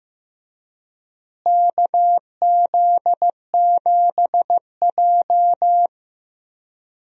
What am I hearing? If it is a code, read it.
Morse code, heard as a keyed tone, about 15 wpm, 700 Hz. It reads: KZ7J
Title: USA Call Signs - 4 Characters - Repeated 3x - 15wpm